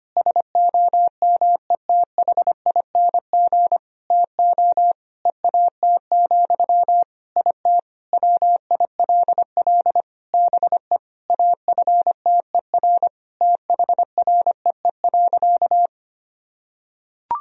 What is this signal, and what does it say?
Morse code, heard as a keyed tone, about 25 wpm, 700 Hz. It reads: SOMETHING TO EAT, IT WILL BE AFTER THREE.
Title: Sentences from Top 1000 Words - Review Part 1 - 25wpm